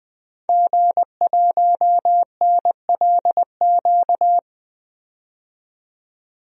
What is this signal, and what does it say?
Morse code, heard as a keyed tone, about 20 wpm, 700 Hz. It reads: G1NLQ
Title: DXCC-100 Call Signs - 5 Characters - 20wpm